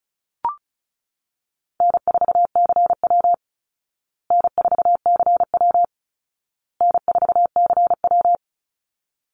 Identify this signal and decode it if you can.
Morse code, heard as a keyed tone, about 34 wpm, 700 Hz. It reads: N4CW N4CW N4CW
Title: Easy Call Signs Repeated 3x - 35wpm